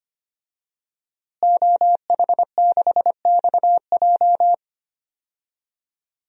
Morse code, heard as a keyed tone, about 25 wpm, 700 Hz. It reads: OH6XJ
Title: DXCC-100 Call Signs - 5 Characters - Repeated 2x - 25wpm